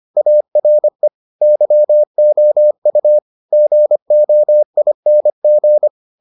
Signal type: Morse code

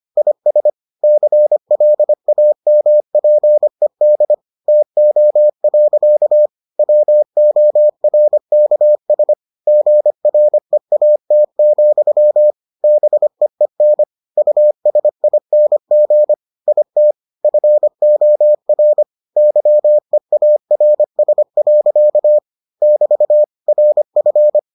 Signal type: Morse code

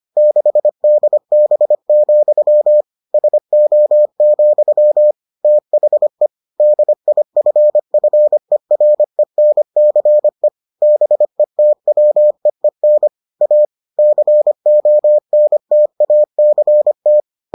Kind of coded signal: Morse code

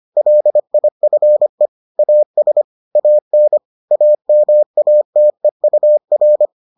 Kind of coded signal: Morse code

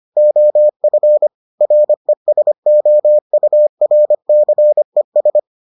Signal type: Morse code

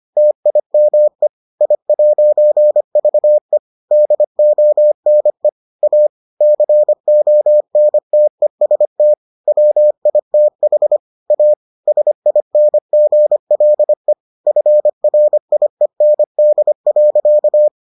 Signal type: Morse code

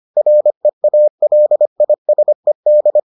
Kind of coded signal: Morse code

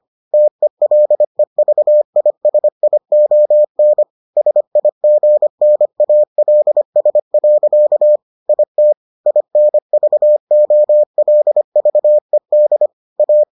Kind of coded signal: Morse code

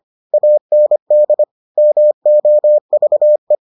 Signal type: Morse code

